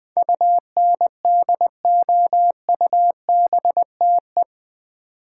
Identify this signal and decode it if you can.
Morse code, heard as a keyed tone, about 20 wpm, 700 Hz. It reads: UNDOUBTE
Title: Build A Word - 4 Letters At A Time - 11-Letter Words - 20wpm